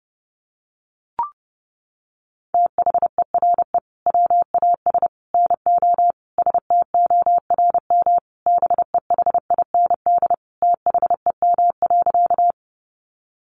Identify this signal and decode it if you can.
Morse code, heard as a keyed tone, about 29 wpm, 700 Hz. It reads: THERE WAS NO STORM BEHIND THEM.
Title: Sentences from Top 1500 Words 30wpm